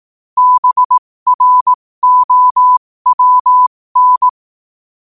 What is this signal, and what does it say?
Morse code, heard as a keyed tone, about 18 wpm, spaced as for 15 wpm, 1000 Hz. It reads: BROWN
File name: qotdfiles / 15wpm